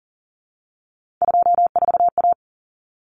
Morse code, heard as a keyed tone, about 40 wpm, 700 Hz. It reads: 24A
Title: Common Cut Numbers - 3-Digit 40wpm